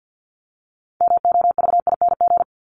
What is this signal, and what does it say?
Morse code, heard as a keyed tone, about 50 wpm, 700 Hz. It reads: MOVING
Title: ING Suffix - Top 100 Words with Base Words 50wpm